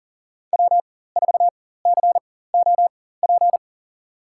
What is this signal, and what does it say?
Morse code, heard as a keyed tone, about 40 wpm, spaced as for 19 wpm, 700 Hz. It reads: W4COP
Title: USA Call Signs - 5 Characters - 4x Character Spacing - 40wpm